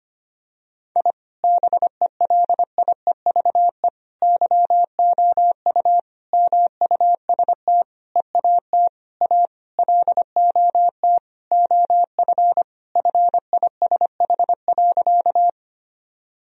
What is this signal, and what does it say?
Morse code, heard as a keyed tone, about 25 wpm, 700 Hz. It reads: I BELIEVE YOU MUST EAT A LOT OF FISH.
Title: Sentences from Top 1000 Words - Review Part 3 - 25wpm